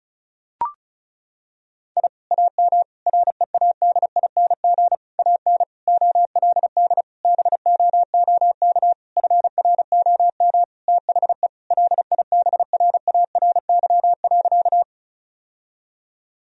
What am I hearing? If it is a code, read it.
Morse code, heard as a keyed tone, about 34 wpm, 700 Hz. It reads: I AM READING AN OLD BOOK FROM THE LIBRARY.
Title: Sentences from Top 1000 Words - Review Part 4 - 35wpm